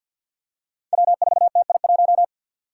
Morse code, heard as a keyed tone, about 47 wpm, 700 Hz. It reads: W4TI1
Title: Five Challenging Letter-Number Combinations 50wpm